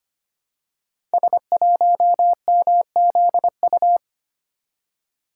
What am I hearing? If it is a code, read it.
Morse code, heard as a keyed tone, about 25 wpm, 700 Hz. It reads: S1MZU
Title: Code Groups - 5 Letter:Numbers - 25wpm